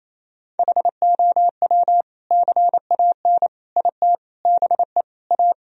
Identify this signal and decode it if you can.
Morse code, heard as a keyed tone, about 28 wpm, 700 Hz. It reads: HOW CAN IT BE A
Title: Sentences from Top 1000 Words - Review Part 2 - 28wpm